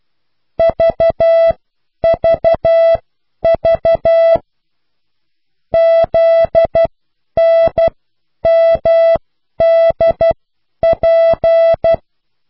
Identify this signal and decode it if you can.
Morse code, heard as a keyed tone, about 12 wpm, spaced as for 9 wpm, 660 Hz. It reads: VVV ZNMDP